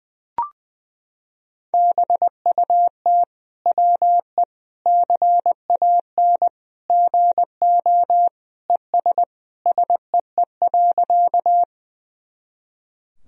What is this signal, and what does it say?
Morse code, heard as a keyed tone, about 20 wpm, 700 Hz. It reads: BUT WE CAN GO ES SEE.
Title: Sentences from Top 100 Words and Abbreviations 20wpm